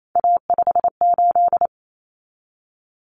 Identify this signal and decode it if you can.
Morse code, heard as a keyed tone, about 28 wpm, 700 Hz. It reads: A58
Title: Common Cut Numbers - 3-Digit 28wpm